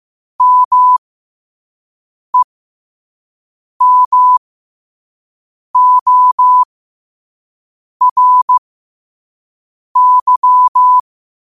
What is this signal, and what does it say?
Morse code, heard as a keyed tone, about 15 wpm, spaced as for 5 wpm, 1000 Hz. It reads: MEMORY
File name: qotdfiles / 5wpm